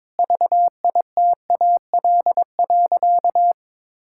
Morse code, heard as a keyed tone, about 22 wpm, 700 Hz. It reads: VITAL.
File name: practice / 22wpm / Sentences with 8-Word Phrases - 22wpm